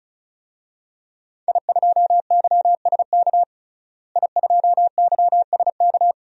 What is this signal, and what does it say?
Morse code, heard as a keyed tone, about 36 wpm, 700 Hz. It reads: I2YSK I2YSK
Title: DXCC-100 Call Signs - 5 Characters - Repeated 2x - 35wpm